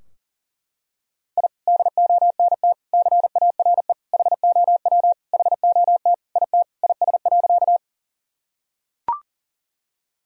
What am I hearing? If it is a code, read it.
Morse code, heard as a keyed tone, about 40 wpm, 700 Hz. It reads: I DONT CARE HOW HOT IT IS.
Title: Sentences from Top 1000 Words - Review Part 3 - 40wpm